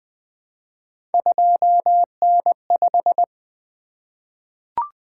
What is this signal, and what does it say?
Morse code, heard as a keyed tone, about 20 wpm, 700 Hz. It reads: 2N5
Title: Three Letter-Number Combinations - 20wpm